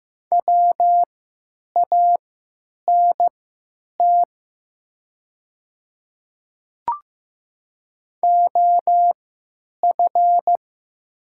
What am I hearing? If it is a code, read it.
Morse code, heard as a keyed tone, about 15 wpm, spaced as for 8 wpm, 700 Hz. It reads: WANT OF
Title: 3x Character Spacing - Top 300 Words - 15wpm